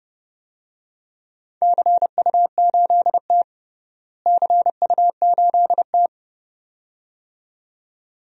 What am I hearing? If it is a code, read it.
Morse code, heard as a keyed tone, about 30 wpm, 700 Hz. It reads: CU8T CU8T
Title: DXCC-100 Call Signs - 4 Characters - Repeated 2x - 30wpm